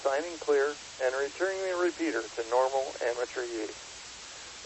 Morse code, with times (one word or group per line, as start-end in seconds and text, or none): none